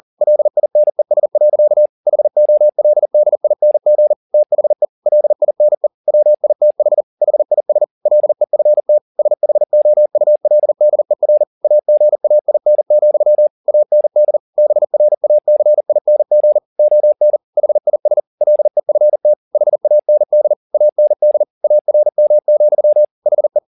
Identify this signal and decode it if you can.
Morse code, heard as a keyed tone, about 40 wpm, 600 Hz. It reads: LINES. HOLDING THE LINE WITH HIS LEFT SHOULDER AGAIN, AND BRACING ON HIS LEFT HAND AND ARM, HE